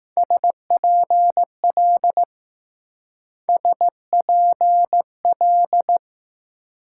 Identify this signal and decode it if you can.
Morse code, heard as a keyed tone, about 19 wpm, 700 Hz. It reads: SPL SPL